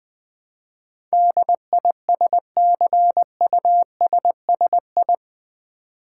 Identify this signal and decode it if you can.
Morse code, heard as a keyed tone, about 21 wpm, 700 Hz. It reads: DISCUSSI
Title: Build A Word - 4 Letters At A Time - 10-Letter Words - 20wpm